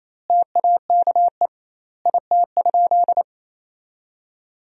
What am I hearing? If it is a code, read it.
Morse code, heard as a keyed tone, about 28 wpm, 700 Hz. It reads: TAKE IT?
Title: 2x Word-Spacing - Sentences from Top 300 Words - 28wpm